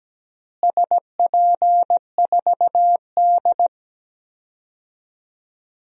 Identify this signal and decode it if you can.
Morse code, heard as a keyed tone, about 17 wpm, 700 Hz. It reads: SP4D
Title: DXCC-100 Call Signs - 4 Characters - Repeated 3x - 17wpm